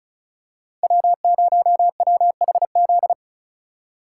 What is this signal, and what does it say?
Morse code, heard as a keyed tone, about 36 wpm, 700 Hz. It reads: W0WHZ
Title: USA Call Signs - 5 Characters - 3x Character Spacing - 35wpm